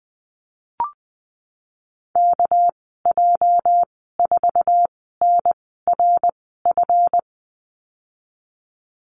Morse code, heard as a keyed tone, about 20 wpm, spaced as for 15 wpm, 700 Hz. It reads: KJ4NRF
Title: 219 Lesson 35g - Number 0 - Call Signs 20wpm